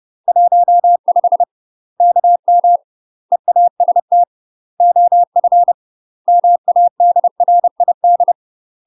Morse code, heard as a keyed tone, about 30 wpm, 700 Hz. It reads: 15 KM EAST OF MADRID
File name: UK MASTER QSOs TRAINING / 30WPM / UK MASTER QSOs 0045